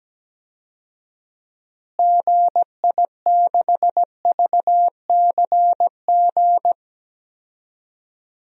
Morse code, heard as a keyed tone, about 17 wpm, 700 Hz. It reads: GI6VCG
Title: DXCC-100 Call Signs - 6 Characters - Repeated 2x - 17wpm